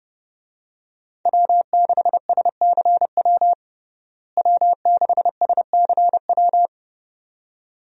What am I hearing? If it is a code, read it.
Morse code, heard as a keyed tone, about 32 wpm, 700 Hz. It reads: W6SCW W6SCW
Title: USA Call Signs - 5 Characters - Repeated 2x - 30wpm